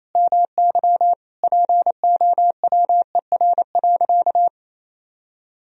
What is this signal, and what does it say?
Morse code, heard as a keyed tone, about 28 wpm, 700 Hz. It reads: MY POWER.
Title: Sentences from Top 1000 Words - Review Part 4 - 28wpm